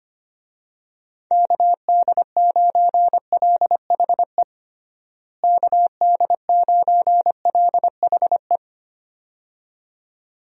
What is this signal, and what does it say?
Morse code, heard as a keyed tone, about 25 wpm, 700 Hz. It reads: KD9LHE KD9LHE
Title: USA Call Signs - 6 Characters - Repeated 2x - 25wpm